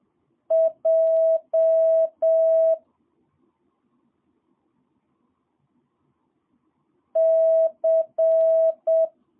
Morse code, heard as a keyed tone, about 7 wpm, 650 Hz. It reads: J C
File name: jh8cbh / 2022cw